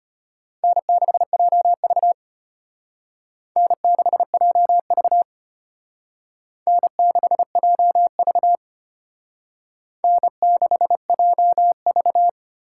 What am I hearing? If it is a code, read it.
Morse code, heard as a keyed tone, about 32 wpm, 700 Hz. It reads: N6JV N6JV N6JV N6JV